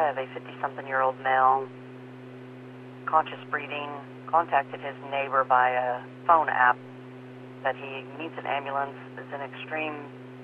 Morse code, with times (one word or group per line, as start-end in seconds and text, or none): none